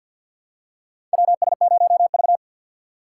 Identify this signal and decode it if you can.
Morse code, heard as a keyed tone, about 49 wpm, 700 Hz. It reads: WS0V
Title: USA Call Signs - 4 Characters - 3x Character Spacing - 50wpm